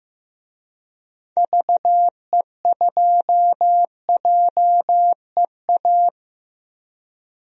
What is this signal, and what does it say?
Morse code, heard as a keyed tone, about 15 wpm, 700 Hz. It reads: VE2JEA
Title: DXCC-100 Call Signs - 6 Characters - Repeated 2x - 15wpm